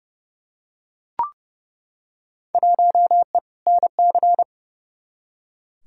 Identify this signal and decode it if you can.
Morse code, heard as a keyed tone, about 30 wpm, 700 Hz. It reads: 1E NC
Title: Field Day Exchanges - 30wpm